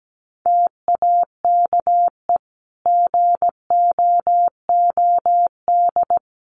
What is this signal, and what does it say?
Morse code, heard as a keyed tone, about 17 wpm, 700 Hz. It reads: TAKE GOOD
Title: Sentences from Top 400 Words 17wpm